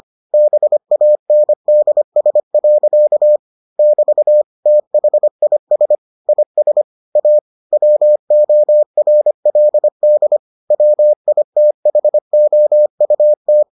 Morse code, heard as a keyed tone, about 25 wpm, 600 Hz. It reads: BANDS. <BT> THIS IS A WORLD WITHOUT